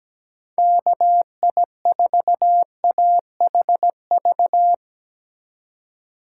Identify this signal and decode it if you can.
Morse code, heard as a keyed tone, about 17 wpm, 700 Hz. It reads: KI4AHV
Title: USA Call Signs - 6 Characters - 2x Character Spacing - 17wpm